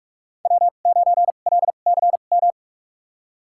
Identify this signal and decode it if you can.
Morse code, heard as a keyed tone, about 44 wpm, spaced as for 32 wpm, 700 Hz. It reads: W9LCM
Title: USA Call Signs - 5 Characters - 2x Character Spacing - 45wpm